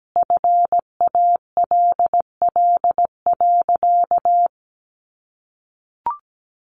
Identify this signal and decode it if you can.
Morse code, heard as a keyed tone, about 17 wpm, 700 Hz. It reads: FALL.
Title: Sentences from Top 500 Words 17wpm